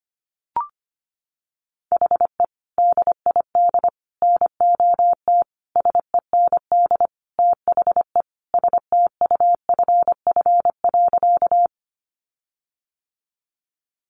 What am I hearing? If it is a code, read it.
Morse code, heard as a keyed tone, about 25 wpm, 700 Hz. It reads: HE DID NOT SEND THE STUFF.
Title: Sentences from Top 1500 Words 25wpm